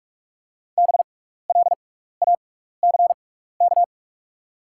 Morse code, heard as a keyed tone, about 46 wpm, spaced as for 15 wpm, 700 Hz. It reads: BLACK
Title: 6x Character Spacing - Top 500 Words - 45wpm